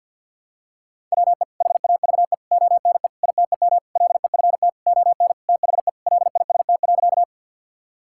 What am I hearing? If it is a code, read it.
Morse code, heard as a keyed tone, about 51 wpm, 700 Hz. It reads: WE HAVE ONE ITEM LEFT ON THE LIST.
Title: Sentences from Top 1000 Words - Review Part 6 - 50wpm